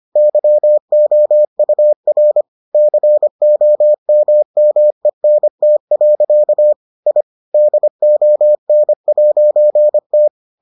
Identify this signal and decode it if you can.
Morse code, heard as a keyed tone, about 25 wpm, 600 Hz. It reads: YOUR COMMENT. I DON'T